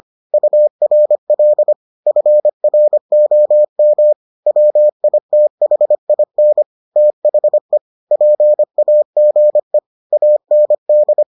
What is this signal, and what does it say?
Morse code, heard as a keyed tone, about 25 wpm, 600 Hz. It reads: URL FROM WITHIN THE PAGE AND